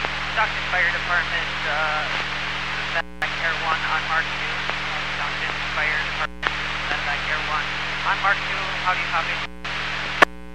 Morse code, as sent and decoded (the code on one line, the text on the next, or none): none